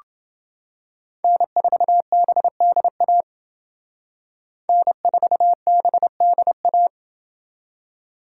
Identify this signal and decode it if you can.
Morse code, heard as a keyed tone, about 28 wpm, 700 Hz. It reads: N4BDA N4BDA